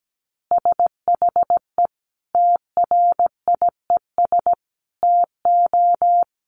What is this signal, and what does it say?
Morse code, heard as a keyed tone, about 17 wpm, 700 Hz. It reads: SHE TRIES TO
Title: Easy Phrases 17wpm